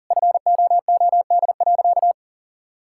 Morse code, heard as a keyed tone, about 40 wpm, 700 Hz. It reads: FOOD.
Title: Sentences from Top 1000 Words - Review Part 5 - 40wpm